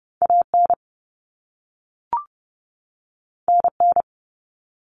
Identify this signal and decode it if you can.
Morse code, heard as a keyed tone, about 30 wpm, 700 Hz. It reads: AN NN
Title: Common Cut Numbers - 2-Digit 30wpm